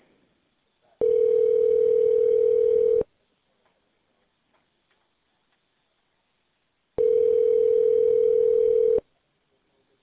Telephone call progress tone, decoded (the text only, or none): ringback tone